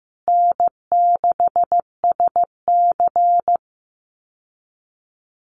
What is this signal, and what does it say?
Morse code, heard as a keyed tone, about 15 wpm, 700 Hz. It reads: N6SC
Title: USA Call Signs Repeated 3x 15wpm